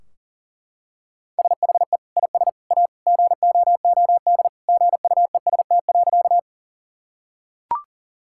Morse code, heard as a keyed tone, about 40 wpm, 700 Hz. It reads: SHE IS A GOOD GUEST.
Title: Sentences from Top 1600 Words 40wpm